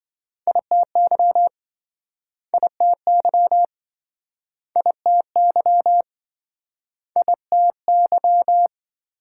Morse code, heard as a keyed tone, about 27 wpm, 700 Hz. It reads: ITY ITY ITY ITY